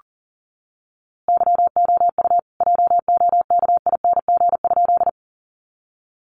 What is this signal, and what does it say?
Morse code, heard as a keyed tone, about 40 wpm, 700 Hz. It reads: YOU JOKING?
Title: A Sound Of Thunder 40wpm